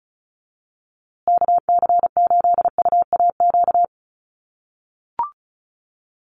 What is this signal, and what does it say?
Morse code, heard as a keyed tone, about 34 wpm, 700 Hz. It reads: KC8UAQ
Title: USA Call Signs Repeated 3x 35wpm